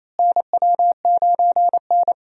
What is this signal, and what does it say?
Morse code, heard as a keyed tone, about 28 wpm, 700 Hz. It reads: NW9N
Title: USA Call Signs - 4 Characters - Repeated 3x - 28wpm